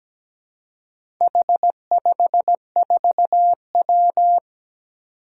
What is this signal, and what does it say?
Morse code, heard as a keyed tone, about 17 wpm, 700 Hz. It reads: H54W
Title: Four Challenging Letter-Number Combinations 17wpm